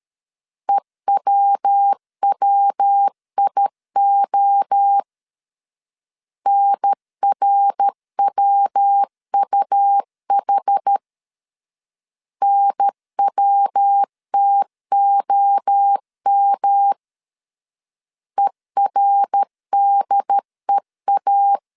Morse code, heard as a keyed tone, about 13 wpm, 800 Hz. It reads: EWWIO NRWUH NWTOM ERDEA
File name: CW / DET / 1 3 1500C S C 12wpm 2750